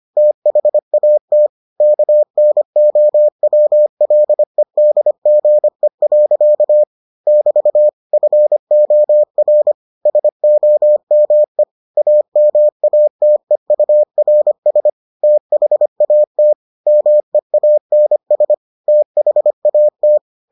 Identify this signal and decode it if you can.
Morse code, heard as a keyed tone, about 25 wpm, 600 Hz. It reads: THAT KNOWLEDGE. <BT> FOR SOME AMATEURS THAT MEANS THAT